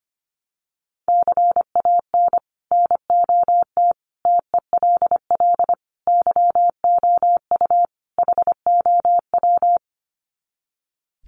Morse code, heard as a keyed tone, about 25 wpm, 700 Hz. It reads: CAN NOT TELL YOU HOW
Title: Sets of 5 Words Encore2 25wpm